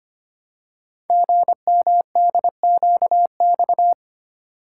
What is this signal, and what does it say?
Morse code, heard as a keyed tone, about 25 wpm, 700 Hz. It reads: GMDQX
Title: Code Groups - 5 Letters - 25wpm